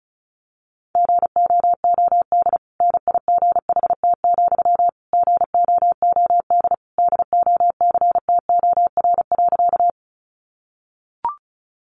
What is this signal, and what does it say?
Morse code, heard as a keyed tone, about 34 wpm, 700 Hz. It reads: GOOD NIGHT, GOOD DOCTOR.
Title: Sentences from Top 1500 Words 35wpm